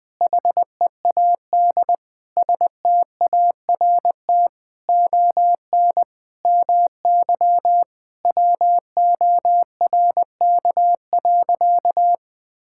Morse code, heard as a keyed tone, about 20 wpm, 700 Hz. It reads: HEAD START ON MY WORK.
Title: Sentences with 7-Word Phrases - 20wpm